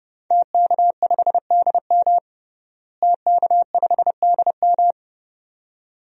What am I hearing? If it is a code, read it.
Morse code, heard as a keyed tone, about 32 wpm, 700 Hz. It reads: TK5DM TK5DM
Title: DXCC-100 Call Signs - 5 Characters - Repeated 2x - 30wpm